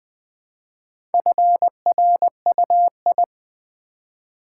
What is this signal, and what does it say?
Morse code, heard as a keyed tone, about 20 wpm, 700 Hz. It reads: FRUI
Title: Build A Word - 1 Letter At A Time - 3 to 5-Letter Words - 20wpm